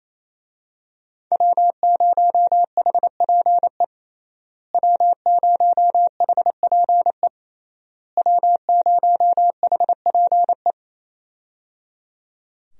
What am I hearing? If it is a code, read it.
Morse code, heard as a keyed tone, about 28 wpm, 700 Hz. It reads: W0HPE W0HPE W0HPE
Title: USA Call Signs - 5 Characters - Repeated 3x - 28wpm